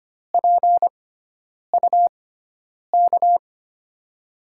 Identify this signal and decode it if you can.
Morse code, heard as a keyed tone, about 25 wpm, spaced as for 9 wpm, 700 Hz. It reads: PUK
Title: USA Call Signs - 6 Characters - 6x Character Spacing - 25wpm